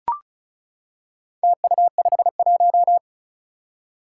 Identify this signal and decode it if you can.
Morse code, heard as a keyed tone, about 35 wpm, 700 Hz. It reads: TU51
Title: Four Challenging Letter-Number Combinations 35wpm